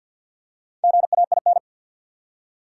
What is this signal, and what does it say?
Morse code, heard as a keyed tone, about 53 wpm, 700 Hz. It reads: GAIN
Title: Four Letter Words - 50wpm